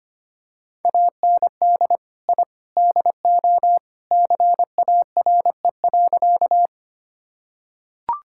Morse code, heard as a keyed tone, about 25 wpm, 700 Hz. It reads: AND I DO CARE.
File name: practice / 25wpm / Sentences with 3-Word Phrases - 25wpm